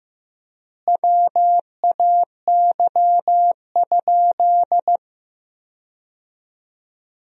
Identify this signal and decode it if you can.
Morse code, heard as a keyed tone, about 15 wpm, 700 Hz. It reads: WAY?
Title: 3x Word-Spacing - Sentences from Top 200 Words - 15wpm